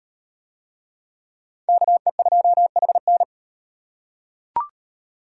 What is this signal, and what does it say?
Morse code, heard as a keyed tone, about 37 wpm, 700 Hz. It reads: KE2HN